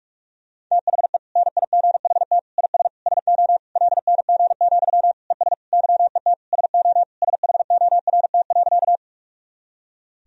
Morse code, heard as a keyed tone, about 41 wpm, 700 Hz. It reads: THE NIGHT IS SO LNG, ES YET SO SHORT.
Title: Sentences from Top 500 Words and Abbreviations 45wpm